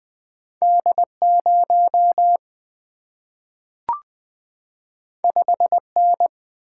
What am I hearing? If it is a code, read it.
Morse code, heard as a keyed tone, about 20 wpm, 700 Hz. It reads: D 0 5 N
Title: Single-Letter Single-Number 20wpm